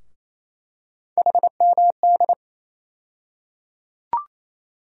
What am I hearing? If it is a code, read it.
Morse code, heard as a keyed tone, about 28 wpm, 700 Hz. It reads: HMD
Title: USA Call Sign Postfixes - 3 Letters 28wpm